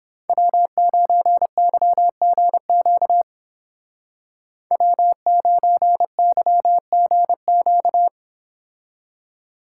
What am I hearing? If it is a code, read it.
Morse code, heard as a keyed tone, about 30 wpm, 700 Hz. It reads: W9YGQ W9YGQ